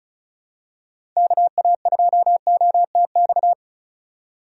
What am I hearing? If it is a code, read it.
Morse code, heard as a keyed tone, about 34 wpm, 700 Hz. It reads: KA2OTX
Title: USA Call Signs - 2x Character Spacing - 35wpm